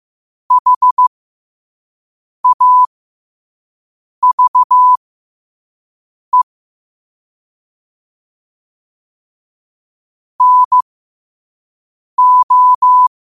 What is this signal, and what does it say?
Morse code, heard as a keyed tone, about 15 wpm, spaced as for 5 wpm, 1000 Hz. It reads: HAVE NO
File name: qotdfiles / 5wpm